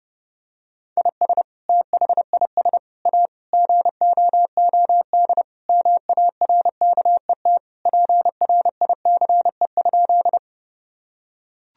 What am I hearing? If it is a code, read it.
Morse code, heard as a keyed tone, about 30 wpm, 700 Hz. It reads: IS THIS A GOOD MARKET PRICE?
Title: Sentences from Top 1000 Words - Review Part 3 - 30wpm